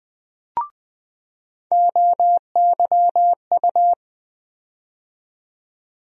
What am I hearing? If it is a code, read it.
Morse code, heard as a keyed tone, about 20 wpm, 700 Hz. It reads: OYU
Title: USA Call Sign Postfixes - 3 Letters 20wpm